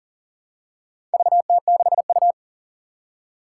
Morse code, heard as a keyed tone, about 39 wpm, 700 Hz. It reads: VT6U
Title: Four Challenging Letter-Number Combinations 40wpm